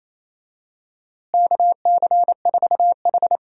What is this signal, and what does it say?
Morse code, heard as a keyed tone, about 28 wpm, 700 Hz. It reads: KC4H
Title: USA Call Signs - 4 Characters - Repeated 2x - 28wpm